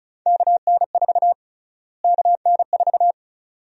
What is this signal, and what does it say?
Morse code, heard as a keyed tone, about 36 wpm, 700 Hz. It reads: KN4 KN4
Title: USA Call Sign Prefixes - Repeated 2x - 35wpm